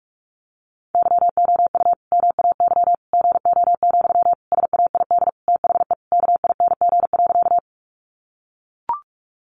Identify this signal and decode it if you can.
Morse code, heard as a keyed tone, about 44 wpm, 700 Hz. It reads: YOU MAY GO, SAID THE KING.
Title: Sentences from Top 400 Words 45wpm